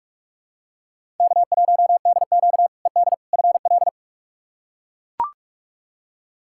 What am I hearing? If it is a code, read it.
Morse code, heard as a keyed tone, about 42 wpm, 700 Hz. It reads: K1DQ ED FL
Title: K1USN SST QSO Practice - 45wpm